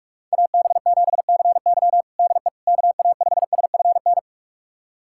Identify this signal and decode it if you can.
Morse code, heard as a keyed tone, about 47 wpm, 700 Hz. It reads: AB7CY DE KA5SFN
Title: USA Callsign DE Callsign 45wpm